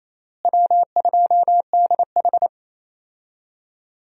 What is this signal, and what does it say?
Morse code, heard as a keyed tone, about 28 wpm, 700 Hz. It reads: W2DH
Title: USA Call Signs - 4 Characters - Repeated 3x - 28wpm